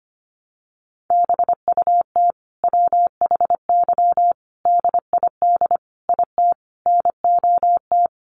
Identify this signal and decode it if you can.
Morse code, heard as a keyed tone, about 25 wpm, 700 Hz. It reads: BUT WHY DID IT NOT
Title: Sentences from Top 1400 Words 25wpm